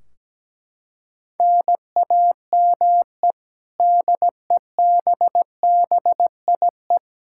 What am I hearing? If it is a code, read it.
Morse code, heard as a keyed tone, about 17 wpm, 700 Hz. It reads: NAME DEBBIE
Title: Practice QSO - Exchange 1 - Beginning - 17wpm